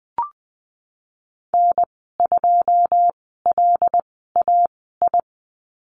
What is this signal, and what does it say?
Morse code, heard as a keyed tone, about 20 wpm, spaced as for 15 wpm, 700 Hz. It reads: N2LAI
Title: 122 Lesson 20g - Number 3 - Call Signs 20wpm